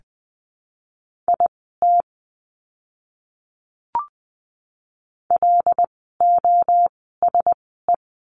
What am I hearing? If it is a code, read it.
Morse code, heard as a keyed tone, about 20 wpm, spaced as for 14 wpm, 700 Hz. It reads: IT LOSE
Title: 108 Lesson 18f - Letter m - Words 20wpm